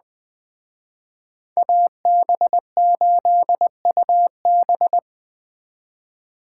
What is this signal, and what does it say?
Morse code, heard as a keyed tone, about 20 wpm, 700 Hz. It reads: AB8UB